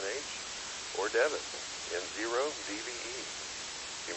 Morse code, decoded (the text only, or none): none